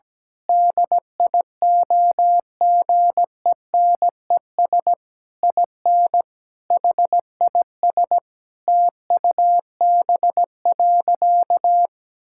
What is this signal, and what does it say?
Morse code, heard as a keyed tone, about 17 wpm, 700 Hz. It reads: DIOGENES IN HIS TUB.
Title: Pangram Challenge v2 - 17wpm